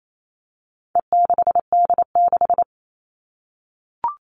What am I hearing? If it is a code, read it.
Morse code, heard as a keyed tone, about 28 wpm, 700 Hz. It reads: E6D6
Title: All Cut Numbers - 4-Digit 28wpm